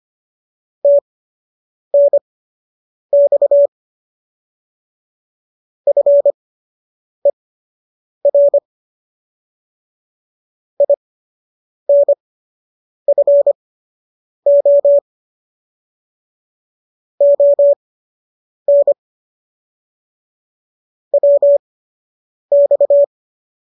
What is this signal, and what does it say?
Morse code, heard as a keyed tone, about 25 wpm, spaced as for 8 wpm, 570 Hz. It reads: TNX FER INFO ON WX